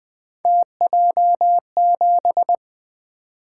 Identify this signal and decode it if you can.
Morse code, heard as a keyed tone, about 20 wpm, 700 Hz. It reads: TJ7
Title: Three Letter-Number Combinations - 20wpm